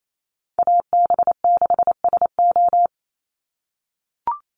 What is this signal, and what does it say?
Morse code, heard as a keyed tone, about 28 wpm, 700 Hz. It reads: AB6SO
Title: NA SOTA Chaser Callsigns 28wpm